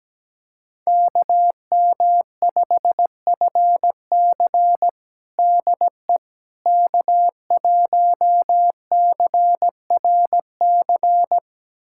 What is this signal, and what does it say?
Morse code, heard as a keyed tone, about 17 wpm, 700 Hz. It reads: KM5FC DE K1CRC
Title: USA Callsign DE Callsign - 5 Characters - 17wpm